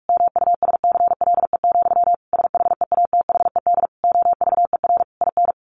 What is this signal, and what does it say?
Morse code, heard as a keyed tone, about 44 wpm, 700 Hz. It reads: MUSCLE, SHEATHED OVER IN
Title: A Sound Of Thunder 45wpm